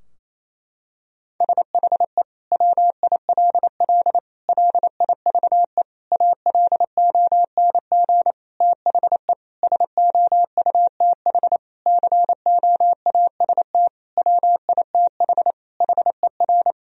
Morse code, heard as a keyed tone, about 27 wpm, 700 Hz. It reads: SHE WILL LIVE ALONG THE SOUTH COAST WITH HER
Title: Sentences from Top 1000 Words - Review Part 4 - 28wpm